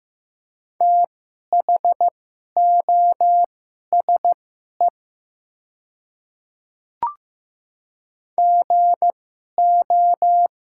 Morse code, heard as a keyed tone, about 15 wpm, spaced as for 11 wpm, 700 Hz. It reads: THOSE GO
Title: 2x Character Spacing - Top 300 Words - 15wpm